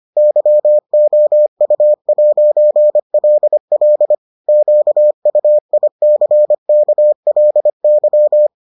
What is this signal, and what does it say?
Morse code, heard as a keyed tone, about 25 wpm, 600 Hz. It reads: YOU'LL QUICKLY